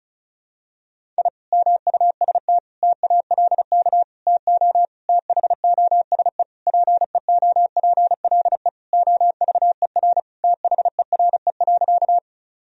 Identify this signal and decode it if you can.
Morse code, heard as a keyed tone, about 36 wpm, 700 Hz. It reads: I MUST TALK TO THOSE PEOPLE OVER THERE.
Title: Sentences from Top 1000 Words - Review Part 1 - 35wpm